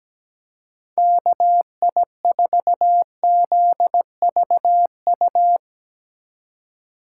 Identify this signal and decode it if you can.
Morse code, heard as a keyed tone, about 17 wpm, 700 Hz. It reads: KI4ZVU
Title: K1USN SST QSO Practice - 17wpm